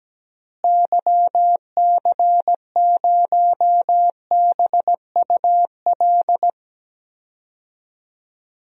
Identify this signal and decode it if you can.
Morse code, heard as a keyed tone, about 17 wpm, 700 Hz. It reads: YC0BUL
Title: DXCC-100 Call Signs - 6 Characters - Repeated 3x - 17wpm